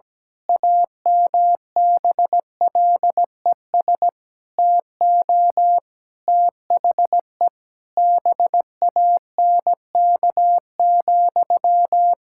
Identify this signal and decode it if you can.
Morse code, heard as a keyed tone, about 17 wpm, 700 Hz. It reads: AMBLES TO THE BANK,
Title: Pangram Challenge v2 - 17wpm